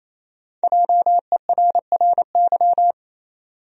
Morse code, heard as a keyed tone, about 27 wpm, 700 Hz. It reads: JERRY
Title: USA Common Names - 28wpm